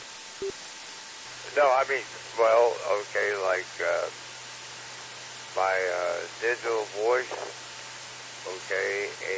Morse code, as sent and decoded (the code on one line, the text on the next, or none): none